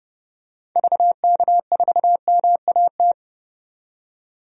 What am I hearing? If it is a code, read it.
Morse code, heard as a keyed tone, about 30 wpm, 700 Hz. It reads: VK4MAT
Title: DXCC-100 Call Signs - 6 Characters - 30wpm